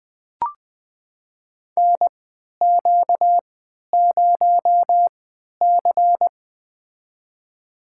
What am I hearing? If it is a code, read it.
Morse code, heard as a keyed tone, about 20 wpm, spaced as for 11 wpm, 700 Hz. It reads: NQ0C
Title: USA Call Signs - 3x Character Spacing - 20wpm